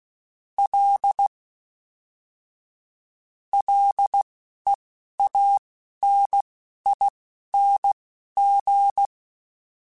Morse code, heard as a keyed tone, about 16 wpm, spaced as for 12 wpm, 800 Hz. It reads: L LEANING